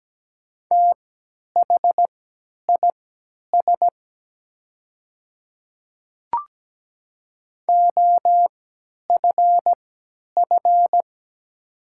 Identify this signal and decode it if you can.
Morse code, heard as a keyed tone, about 17 wpm, spaced as for 10 wpm, 700 Hz. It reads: THIS OFF
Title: 3x Character Spacing - Top 200 Words - 17wpm